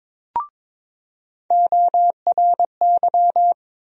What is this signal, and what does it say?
Morse code, heard as a keyed tone, about 22 wpm, 700 Hz. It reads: ORY
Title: USA Call Sign Postfixes - 3 Letters 22wpm